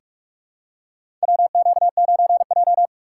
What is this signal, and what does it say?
Morse code, heard as a keyed tone, about 47 wpm, 700 Hz. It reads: WQ9J
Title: USA Call Signs - 4 Characters - 3x Character Spacing - 45wpm